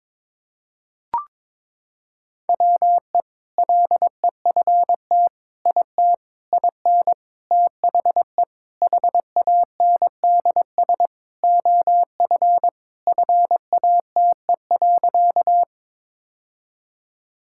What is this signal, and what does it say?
Morse code, heard as a keyed tone, about 22 wpm, 700 Hz. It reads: WE LEFT IT IN THE HANDS OF FATE.
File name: practice / 22wpm / Sentences with 4-Word Phrases - 22wpm